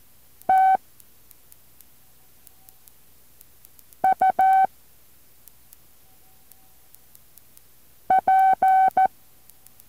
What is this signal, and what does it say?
Morse code, heard as a keyed tone, about 14 wpm, 760 Hz. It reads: T U P
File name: intl morse code2